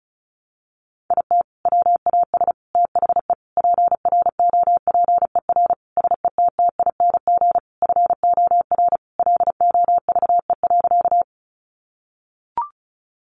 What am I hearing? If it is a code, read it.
Morse code, heard as a keyed tone, about 34 wpm, 700 Hz. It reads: IT WAS THE PROPER SETTING FOR LOVE.
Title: Sentences from Top 1300 Words 35wpm